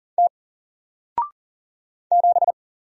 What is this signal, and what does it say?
Morse code, heard as a keyed tone, about 40 wpm, 700 Hz. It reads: T 7
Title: Single Letter-Number 40wpm